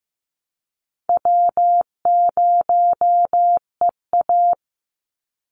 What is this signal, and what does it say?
Morse code, heard as a keyed tone, about 15 wpm, 700 Hz. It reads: W0EA
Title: Call Signs Repeated 2x - 15wpm